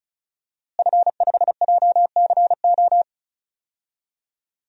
Morse code, heard as a keyed tone, about 34 wpm, 700 Hz. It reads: F5JCO
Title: DXCC-100 Call Signs - 5 Characters - Repeated 3x - 35wpm